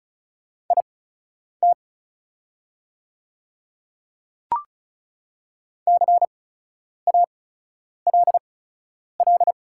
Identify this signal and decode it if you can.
Morse code, heard as a keyed tone, about 34 wpm, spaced as for 10 wpm, 700 Hz. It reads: IT CALL
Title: 8x Character Spacing - Top 100 Words - 35wpm